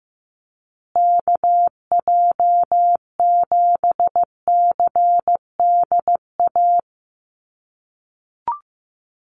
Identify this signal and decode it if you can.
Morse code, heard as a keyed tone, about 15 wpm, 700 Hz. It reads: KJ7CDA
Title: USA Call Signs Repeated 2x 15wpm